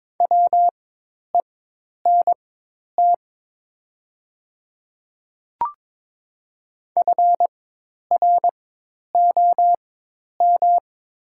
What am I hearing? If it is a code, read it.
Morse code, heard as a keyed tone, about 22 wpm, spaced as for 10 wpm, 700 Hz. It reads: WENT FROM
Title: 4x Character Spacing - Top 200 Words - 22wpm